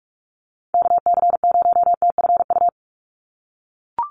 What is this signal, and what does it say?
Morse code, heard as a keyed tone, about 44 wpm, 700 Hz. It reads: KC0TFU
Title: USA Call Signs 45wpm